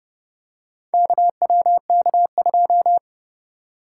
Morse code, heard as a keyed tone, about 29 wpm, 700 Hz. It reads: KWK2